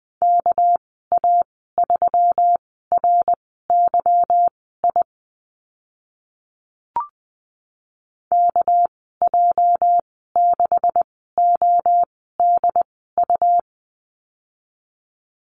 Morse code, heard as a keyed tone, about 20 wpm, spaced as for 14 wpm, 700 Hz. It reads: KA3RYI KJ6ODU
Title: 213 Lesson 34g - Number 8 - Call Signs 20wpm